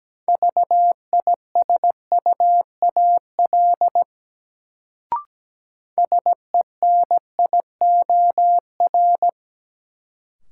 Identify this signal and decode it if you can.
Morse code, heard as a keyed tone, about 17 wpm, 700 Hz. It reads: VISUAL SENIOR
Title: Sets of 5 Words Encore - New Word Drill - 17wpm